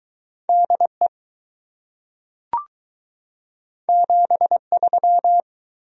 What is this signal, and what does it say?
Morse code, heard as a keyed tone, about 20 wpm, 700 Hz. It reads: DE 73